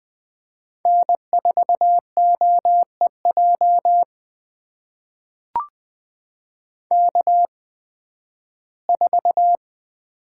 Morse code, heard as a keyed tone, about 20 wpm, spaced as for 13 wpm, 700 Hz. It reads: N4OEJ K4
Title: USA Call Signs - 8x Character Spacing - 20wpm